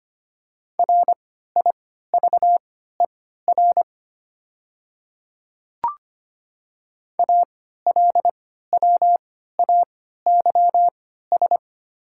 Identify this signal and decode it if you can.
Morse code, heard as a keyed tone, about 25 wpm, spaced as for 14 wpm, 690 Hz. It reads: RIVER ALWAYS
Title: 3x Character Spacing - Top 300 Words - 25wpm